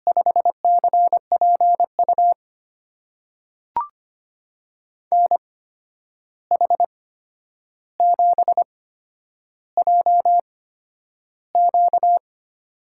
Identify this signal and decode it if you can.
Morse code, heard as a keyed tone, about 25 wpm, spaced as for 10 wpm, 700 Hz. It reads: 5CPU NH7JQ
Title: USA Call Signs - 8x Character Spacing - 25wpm